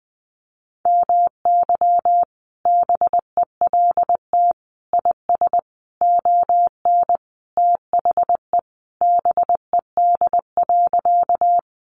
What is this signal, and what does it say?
Morse code, heard as a keyed tone, about 20 wpm, 700 Hz. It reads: MY BELT IS ON THE BED.
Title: Sentences from Top 1400 Words 20wpm